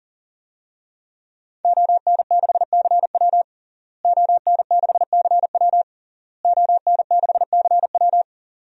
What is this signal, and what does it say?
Morse code, heard as a keyed tone, about 40 wpm, 700 Hz. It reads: ON6CW ON6CW ON6CW
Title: DXCC-100 Call Signs - 5 Characters - Repeated 3x - 40wpm